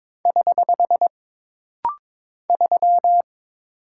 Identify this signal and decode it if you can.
Morse code, heard as a keyed tone, about 22 wpm, 700 Hz. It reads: <HH> 3
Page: This rapid-fire timing is geared to push your ICR (Instant Character Recognition) skills.